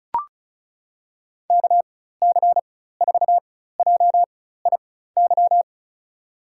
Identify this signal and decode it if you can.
Morse code, heard as a keyed tone, about 36 wpm, spaced as for 16 wpm, 700 Hz. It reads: KC4JIY
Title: USA Call Signs - 4x Character Spacing - 35wpm